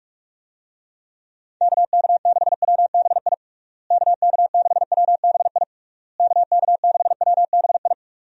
Morse code, heard as a keyed tone, about 46 wpm, 700 Hz. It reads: KK6WBI KK6WBI KK6WBI
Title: USA Call Signs - 6 Characters - Repeated 3x - 45wpm